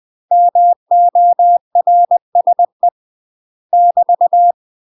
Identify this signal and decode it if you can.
Morse code, heard as a keyed tone, about 20 wpm, 700 Hz. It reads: MORSE <BT>
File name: UK MASTER QSOs TRAINING / 20WPM / UK MASTER QSOs 0068